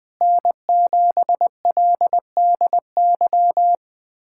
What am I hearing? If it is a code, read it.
Morse code, heard as a keyed tone, about 21 wpm, 700 Hz. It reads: N7LDY
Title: USA Callsign DE Callsign 20wpm